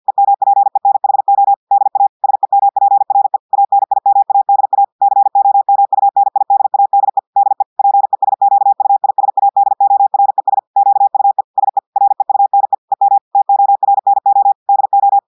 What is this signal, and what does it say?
Morse code, heard as a keyed tone, about 50 wpm, 800 Hz. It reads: EMPENHO DA SEMPRE ANIMADA COMUNIDADE DE PESQUISADORES QUE SE REUNE EM TORNO DO
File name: audios / portugais / portugais texte en clair 1 vitesse 50 mots minutes